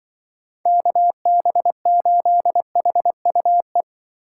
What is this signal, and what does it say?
Morse code, heard as a keyed tone, about 25 wpm, 700 Hz. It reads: KB8HUE
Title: Speed Racing - USA 2x3 Call Signs 20wpm